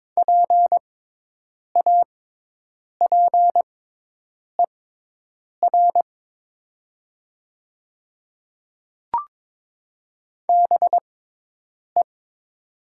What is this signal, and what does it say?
Morse code, heard as a keyed tone, about 21 wpm, spaced as for 8 wpm, 700 Hz. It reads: PAPER BE
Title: 6x Character Spacing - Top 300 Words - 22wpm